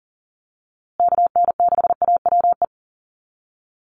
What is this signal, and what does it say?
Morse code, heard as a keyed tone, about 40 wpm, 700 Hz. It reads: KN6AWE
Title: USA Call Signs Repeated 3x 40wpm